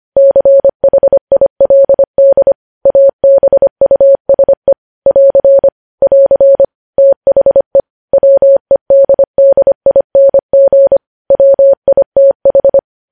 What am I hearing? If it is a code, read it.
Morse code, heard as a keyed tone, about 25 wpm, 550 Hz. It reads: CHILD ABUSE <AR> <AR> THE WEDDING WITH
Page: news headlines in morse code at 25 wpm